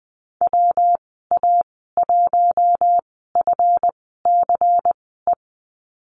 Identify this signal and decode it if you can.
Morse code, heard as a keyed tone, about 20 wpm, spaced as for 15 wpm, 700 Hz. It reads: WA1FCE
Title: 231 Lesson 37g - Letter x - Call Signs 20wpm